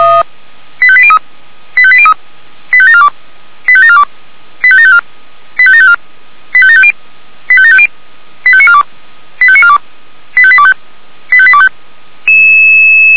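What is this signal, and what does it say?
Five-tone selective call (ZVEI), heard as quick runs of five tones, tones of about 70 ms. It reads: 85803 85803 85742 85742 85754 85754 85750 85750 85942 85942 85825 85825
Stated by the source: ZVEI five-tone call received by SDR.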